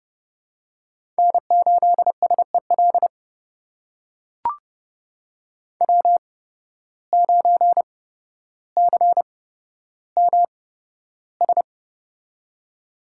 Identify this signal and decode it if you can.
Morse code, heard as a keyed tone, about 30 wpm, spaced as for 13 wpm, 700 Hz. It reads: N8SEL W9CMS
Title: USA Call Signs - 5 Characters - 8x Character Spacing - 30wpm